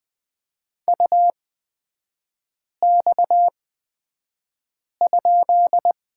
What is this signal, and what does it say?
Morse code, heard as a keyed tone, about 20 wpm, spaced as for 5 wpm, 700 Hz. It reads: UX?
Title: 236 Lesson 38e - Letter q - Three Characters 20wpm